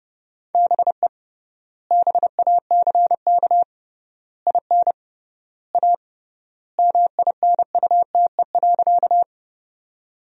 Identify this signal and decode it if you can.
Morse code, heard as a keyed tone, about 32 wpm, 700 Hz. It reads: BE BACK IN A MINUTE.
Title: 3x Word-Spacing - Sentences from Top 500 Words - 30wpm